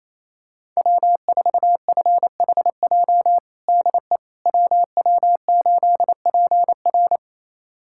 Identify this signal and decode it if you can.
Morse code, heard as a keyed tone, about 27 wpm, 700 Hz. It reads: W4FHJ DE WW8PR
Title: USA Callsign DE Callsign - 5 Characters - 28wpm